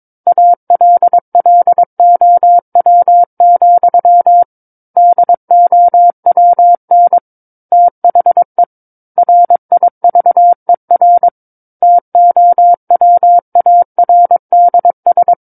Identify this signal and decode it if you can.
Morse code, heard as a keyed tone, about 22 wpm, 700 Hz. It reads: ALLOW, DOWN THE RIVER TOWARDS
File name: Book One / 13-Chapter 13 - 22 WPM